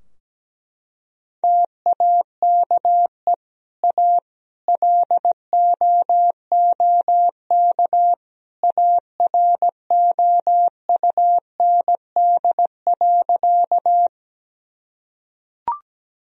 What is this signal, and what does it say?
Morse code, heard as a keyed tone, about 17 wpm, 700 Hz. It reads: TAKE A LOOK AROUND.
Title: Sentences from Top 1000 Words - Review Part 3 - 17wpm